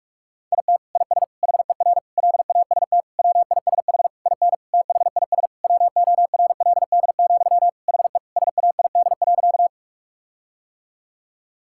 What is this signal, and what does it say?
Morse code, heard as a keyed tone, about 46 wpm, 700 Hz. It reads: IT IS HER LAST WISH IN THIS WORLD, HE SAID.
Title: Sentences from Top 1600 Words 45wpm